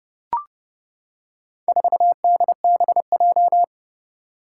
Morse code, heard as a keyed tone, about 31 wpm, 700 Hz. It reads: 4DBJ
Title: Four Challenging Letter-Number Combinations 30wpm